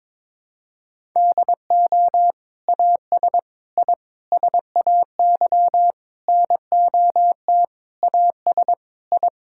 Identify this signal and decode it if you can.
Morse code, heard as a keyed tone, about 22 wpm, 700 Hz. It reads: DO AS I SAY NOT AS I
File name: practice / 22wpm / Sets of 8 Words - 22wpm